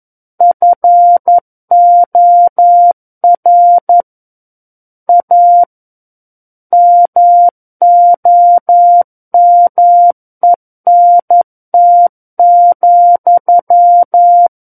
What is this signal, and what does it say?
Morse code, heard as a keyed tone, about 11 wpm, 700 Hz. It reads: FOR A MOMENT,
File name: Book One / 02-Chapter 2 - 11 WPM